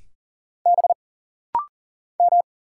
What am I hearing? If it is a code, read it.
Morse code, heard as a keyed tone, about 40 wpm, 700 Hz. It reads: B M
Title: Single Letter-Number - Rapid-Fire 40wpm